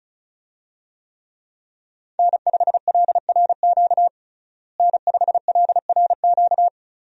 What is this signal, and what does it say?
Morse code, heard as a keyed tone, about 36 wpm, 700 Hz. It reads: N5LRQ N5LRQ
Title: USA Call Signs - 5 Characters - Repeated 2x - 35wpm